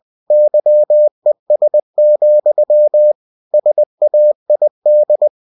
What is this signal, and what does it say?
Morse code, heard as a keyed tone, about 20 wpm, 600 Hz. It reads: YES, SAID
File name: cw-books / winnie-the-pooh / winnie-20wpm-page-0096